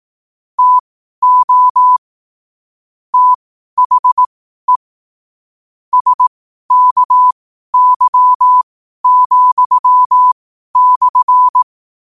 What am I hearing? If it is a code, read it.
Morse code, heard as a keyed tone, about 18 wpm, spaced as for 12 wpm, 1000 Hz. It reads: TO THE SKY,/
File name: qotdfiles / 12wpm